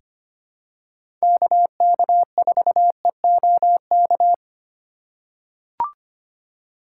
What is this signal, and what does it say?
Morse code, heard as a keyed tone, about 25 wpm, 700 Hz. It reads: KK4EOK
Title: USA Call Signs - 6 Characters - 3x Character Spacing - 25wpm